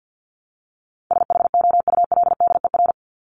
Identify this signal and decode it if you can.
Morse code, heard as a keyed tone, about 48 wpm, 700 Hz. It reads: SHOULDER
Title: Top 1501-1750 Words 50wpm